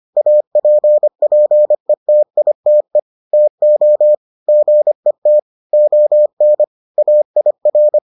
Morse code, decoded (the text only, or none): APPETITE TO GET ON AIR